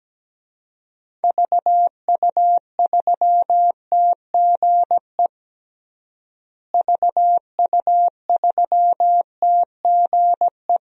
Morse code, heard as a keyed tone, about 17 wpm, 700 Hz. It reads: VU3TGE VU3TGE
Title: DXCC-100 Call Signs - 6 Characters - Repeated 2x - 17wpm